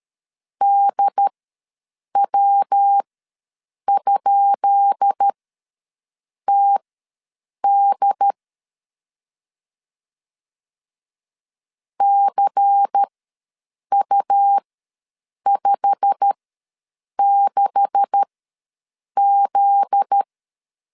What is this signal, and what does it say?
Morse code, heard as a keyed tone, about 13 wpm, spaced as for 7 wpm, 800 Hz. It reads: DW?TD CU56Z
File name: LEZIONI / 10 SIGNS / 10C DOT ws 12wpm 3548